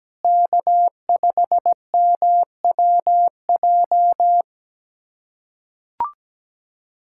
Morse code, heard as a keyed tone, about 17 wpm, 700 Hz. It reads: K5MWJ
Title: USA Callsign DE Callsign - 5 Characters - 17wpm